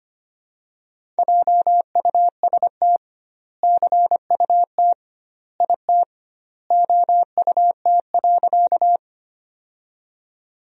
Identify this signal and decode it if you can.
Morse code, heard as a keyed tone, about 25 wpm, 700 Hz. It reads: JUST CUT IT OUT.
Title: 2x Word-Spacing - Sentences from Top 300 Words - 25wpm